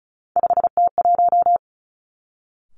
Morse code, heard as a keyed tone, about 36 wpm, 700 Hz. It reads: 5T1
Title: Three Challenging Letter-Number Combinations 35wpm